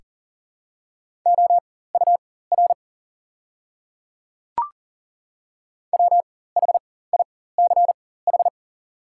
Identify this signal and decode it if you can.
Morse code, heard as a keyed tone, about 40 wpm, spaced as for 19 wpm, 700 Hz. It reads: OUR WHICH
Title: 4x Character Spacing - Top 300 Words - 40wpm